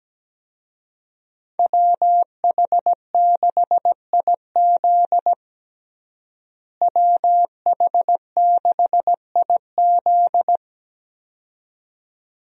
Morse code, heard as a keyed tone, about 17 wpm, 700 Hz. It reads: WH6IZ WH6IZ
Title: USA Call Signs - 5 Characters - Repeated 2x - 17wpm